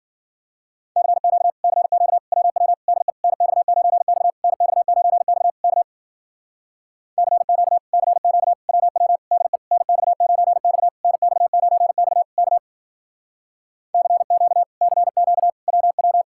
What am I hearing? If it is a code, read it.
Morse code, heard as a keyed tone, about 51 wpm, 700 Hz. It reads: CQ CQ WW DE NX9X NX9X K CQ CQ WW DE NX9X NX9X K CQ CQ WW